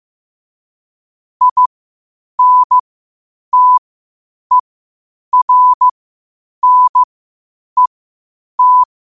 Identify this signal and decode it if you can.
Morse code, heard as a keyed tone, about 15 wpm, spaced as for 8 wpm, 1000 Hz. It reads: INTERNET
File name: qotdfiles / 8wpm